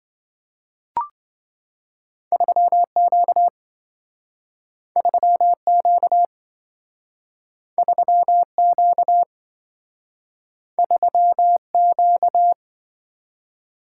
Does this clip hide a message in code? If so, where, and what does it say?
Yes: Morse code from 2.3 to 12.5 s, heard as a keyed tone, about 27 wpm, 700 Hz. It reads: 3Q 3Q 3Q 3Q